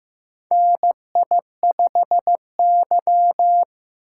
Morse code, heard as a keyed tone, about 15 wpm, 700 Hz. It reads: NI5Y
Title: USA Call Signs - 4 Characters - 6x Character Spacing - 15wpm